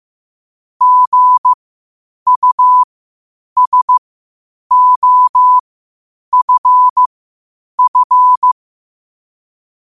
Morse code, heard as a keyed tone, about 15 wpm, spaced as for 8 wpm, 1000 Hz. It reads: GUSOFF
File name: qotdfiles / 8wpm